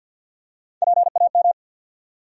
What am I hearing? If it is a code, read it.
Morse code, heard as a keyed tone, about 53 wpm, 700 Hz. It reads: PAM
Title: USA Common Names - Max 5 Letters - 50wpm